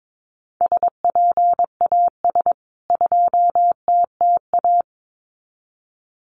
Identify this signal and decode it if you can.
Morse code, heard as a keyed tone, about 22 wpm, 700 Hz. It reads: SPAS 2TTA
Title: CWT Exchange 22wpm